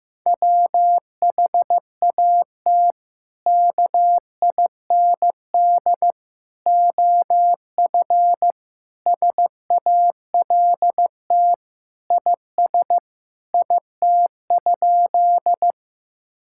Morse code, heard as a keyed tone, about 15 wpm, 700 Hz. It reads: WHAT KIND OF SALT IS IT?
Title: Sentences from Top 1000 Words - Review Part 5 - 15wpm